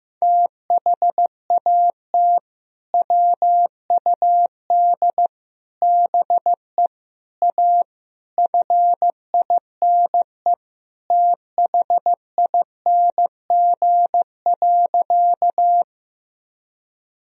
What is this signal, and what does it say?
Morse code, heard as a keyed tone, about 16 wpm, 700 Hz. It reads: THAT WUD BE A FINE THING.
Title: Sentences from Top 500 Words and Abbreviations 15wpm